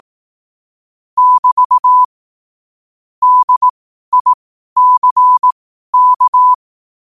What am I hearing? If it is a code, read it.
Morse code, heard as a keyed tone, about 18 wpm, spaced as for 12 wpm, 1000 Hz. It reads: <BT> DICK
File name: qotdfiles / 12wpm